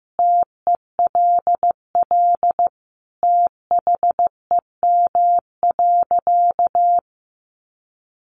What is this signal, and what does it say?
Morse code, heard as a keyed tone, about 15 wpm, 700 Hz. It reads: TELL THEM.
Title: Sentences from Top 900 Words 15wpm